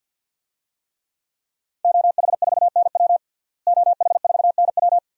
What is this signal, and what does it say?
Morse code, heard as a keyed tone, about 47 wpm, 700 Hz. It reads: OH4NW OH4NW
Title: DXCC-100 Call Signs - 5 Characters - Repeated 3x - 50wpm